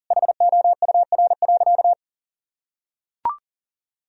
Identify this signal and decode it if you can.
Morse code, heard as a keyed tone, about 40 wpm, 700 Hz. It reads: HOUR.
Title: Sentences from Top 1000 Words - Review Part 3 - 40wpm